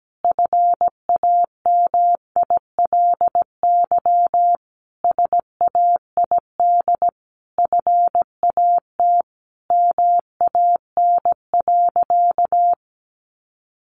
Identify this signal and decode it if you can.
Morse code, heard as a keyed tone, about 17 wpm, 700 Hz. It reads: FAMILY SAID FAT MAN.
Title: Sentences from Top 1100 Words 17wpm